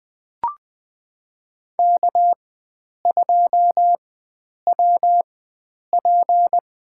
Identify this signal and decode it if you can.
Morse code, heard as a keyed tone, about 21 wpm, spaced as for 9 wpm, 700 Hz. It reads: K2WP
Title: USA Call Signs - 4 Characters - 4x Character Spacing - 20wpm